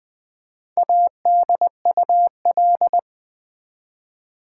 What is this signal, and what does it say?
Morse code, heard as a keyed tone, about 20 wpm, 700 Hz. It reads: ADUL
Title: Build A Word - 1 Letter At A Time - 3 to 5-Letter Words - 20wpm